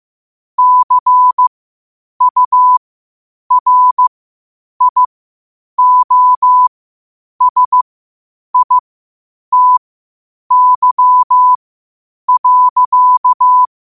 Morse code, heard as a keyed tone, about 15 wpm, spaced as for 8 wpm, 1000 Hz. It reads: CURIOSITY.
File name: qotdfiles / 8wpm